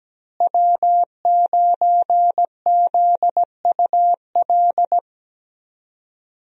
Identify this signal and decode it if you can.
Morse code, heard as a keyed tone, about 17 wpm, 700 Hz. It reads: W9ZUL
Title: USA Call Signs - 5 Characters - Repeated 3x - 17wpm